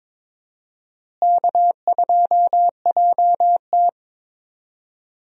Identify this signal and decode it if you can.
Morse code, heard as a keyed tone, about 22 wpm, 700 Hz. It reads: K2JT
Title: USA Call Signs - 4 Characters - Repeated 3x - 22wpm